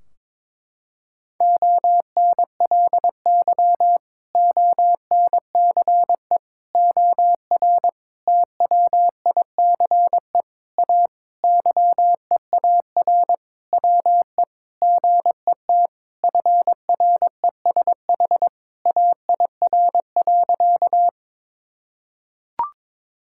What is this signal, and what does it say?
Morse code, heard as a keyed tone, about 21 wpm, 700 Hz. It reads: ONLY ONCE OR TWICE A YEAR WE GET FRESH AIR.
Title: Sentences from Top 1600 Words 22wpm